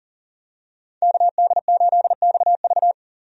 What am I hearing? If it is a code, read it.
Morse code, heard as a keyed tone, about 40 wpm, 700 Hz. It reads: KD8XV
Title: USA Call Signs - 5 Characters - 40wpm